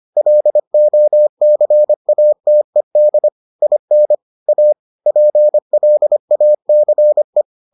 Morse code, heard as a keyed tone, about 25 wpm, 600 Hz. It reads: LOCATED IN A PLACE